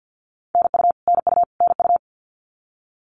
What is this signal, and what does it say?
Morse code, heard as a keyed tone, about 50 wpm, 700 Hz. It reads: NU NU NU
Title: Canadian Provinces and Territories - Repeated 3x - 50wpm